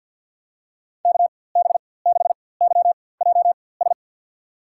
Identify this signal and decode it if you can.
Morse code, heard as a keyed tone, about 46 wpm, spaced as for 23 wpm, 700 Hz. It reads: KB6YJS
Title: USA Call Signs - 6 Characters - 4x Character Spacing - 50wpm